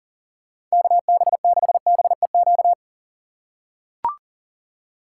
Morse code, heard as a keyed tone, about 40 wpm, 700 Hz. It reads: KB6BEQ
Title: USA Call Signs - 6 Characters - 6x Character Spacing - 40wpm